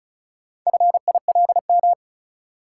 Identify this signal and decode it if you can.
Morse code, heard as a keyed tone, about 36 wpm, 700 Hz. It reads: FILM
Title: Sentences from Top 1000 Words - Review Part 3 - New Word Drill - 35wpm